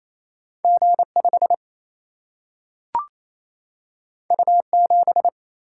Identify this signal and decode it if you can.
Morse code, heard as a keyed tone, about 28 wpm, 700 Hz. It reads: G 5 U 7
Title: Single-Letter Single-Number 28wpm